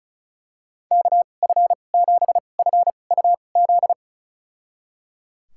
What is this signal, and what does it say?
Morse code, heard as a keyed tone, about 34 wpm, spaced as for 25 wpm, 700 Hz. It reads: KF7FUZ
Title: USA Call Signs - 6 Characters - 2x Character Spacing - 35wpm